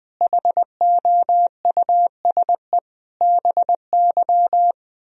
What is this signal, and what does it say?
Morse code, heard as a keyed tone, about 20 wpm, 700 Hz. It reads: HOUSE BY
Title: Sentences from Top 1000 Words - Review Part 4 - 20wpm